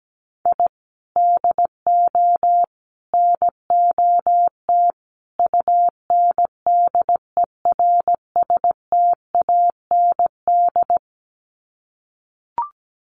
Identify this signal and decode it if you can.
Morse code, heard as a keyed tone, about 17 wpm, 700 Hz. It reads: I DO NOT UNDERSTAND
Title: Sets of 4 Words 17wpm